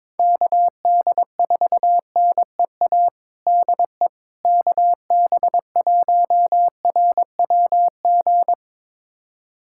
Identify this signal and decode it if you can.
Morse code, heard as a keyed tone, about 22 wpm, 700 Hz. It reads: KD4NEA DE KB1RWG
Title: USA Callsign DE Callsign - 6 Characters - 22wpm